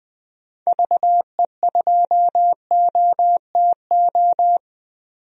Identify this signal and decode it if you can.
Morse code, heard as a keyed tone, about 20 wpm, 700 Hz. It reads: VE2OTO
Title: DXCC-100 Call Signs - 6 Characters - 20wpm